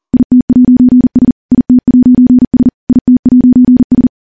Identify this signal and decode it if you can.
Morse code, heard as a keyed tone, about 42 wpm, 250 Hz. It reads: IT'S IT'S IT'S
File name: morse / morse-training-40WPM